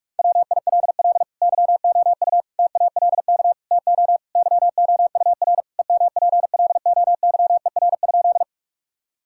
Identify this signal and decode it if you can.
Morse code, heard as a keyed tone, about 44 wpm, 700 Hz. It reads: WILL YOU TALK TO YOUR EMPLOYER?
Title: Sentences from Top 1600 Words 45wpm